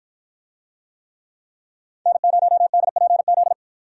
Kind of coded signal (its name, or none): Morse code